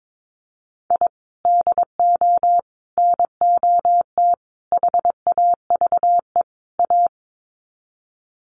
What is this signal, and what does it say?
Morse code, heard as a keyed tone, about 22 wpm, 700 Hz. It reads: I DO NOT HAVE A
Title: Sets of 5 Words 22wpm